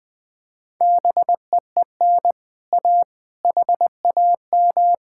Morse code, heard as a keyed tone, about 21 wpm, 700 Hz. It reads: BEEN A HAM
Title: Speed-Racing Common QSO Phrases - 20wpm